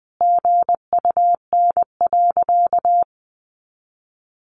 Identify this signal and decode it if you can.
Morse code, heard as a keyed tone, about 20 wpm, 700 Hz. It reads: GUN.
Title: A Sound Of Thunder 20wpm